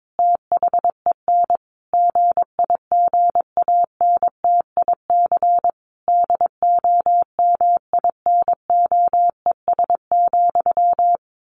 Morse code, heard as a keyed tone, about 22 wpm, 700 Hz. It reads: THEN GIGANTIC DOMINOES,
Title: A Sound Of Thunder 22wpm